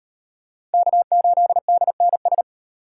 Morse code, heard as a keyed tone, about 37 wpm, 700 Hz. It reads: K8DNS